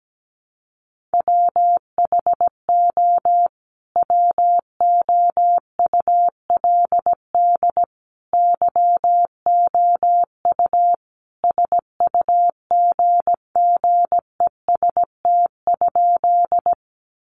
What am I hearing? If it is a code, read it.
Morse code, heard as a keyed tone, about 17 wpm, 700 Hz. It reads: WHO WOULD YOU SUGGEST?
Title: Sentences from Top 1300 Words 17wpm